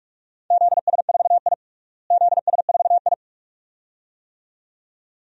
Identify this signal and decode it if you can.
Morse code, heard as a keyed tone, about 47 wpm, 700 Hz. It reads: ZS4I ZS4I
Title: DXCC-100 Call Signs - 4 Characters - Repeated 2x - 45wpm